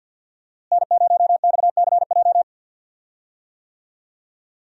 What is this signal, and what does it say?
Morse code, heard as a keyed tone, about 46 wpm, 700 Hz. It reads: N0XCJ
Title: USA Call Signs - 5 Characters - Repeated 3x - 50wpm